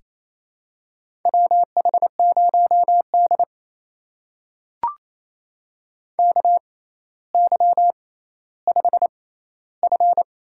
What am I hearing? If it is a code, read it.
Morse code, heard as a keyed tone, about 28 wpm, spaced as for 14 wpm, 700 Hz. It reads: WH0D KY5F
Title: USA Call Signs - 4 Characters - 6x Character Spacing - 28wpm